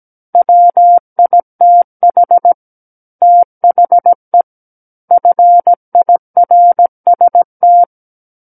Morse code, heard as a keyed tone, about 17 wpm, 700 Hz. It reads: WITH THE FIRST
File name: Book One / 08-Chapter 8 - 17 WPM